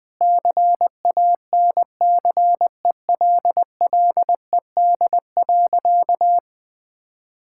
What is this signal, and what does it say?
Morse code, heard as a keyed tone, about 20 wpm, 700 Hz. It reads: CANCELLED.
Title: Sentences with 8-Word Phrases - 20wpm